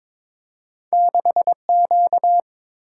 Morse code, heard as a keyed tone, about 22 wpm, 690 Hz. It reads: 6Q
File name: practice / 22wpm / Two Letter-Number Combinations - 22wpm